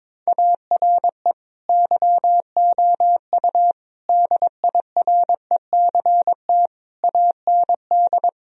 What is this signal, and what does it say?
Morse code, heard as a keyed tone, about 22 wpm, 700 Hz. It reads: ARE YOU DIRECT AND
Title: Sentences from Top 1000 Words - Review Part 4 - 22wpm